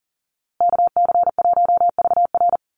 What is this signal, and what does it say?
Morse code, heard as a keyed tone, about 40 wpm, 700 Hz. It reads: KC1VR
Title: USA Call Signs Repeated 3x 40wpm